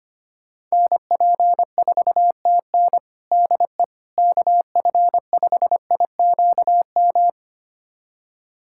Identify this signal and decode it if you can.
Morse code, heard as a keyed tone, about 25 wpm, 700 Hz. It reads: NP4TN DE KF5IQM
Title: USA Callsign DE Callsign 25wpm